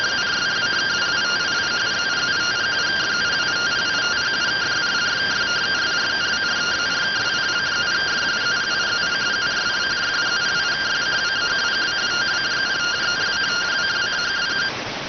Two-tone teleprinter signal, about 170 Hz shift.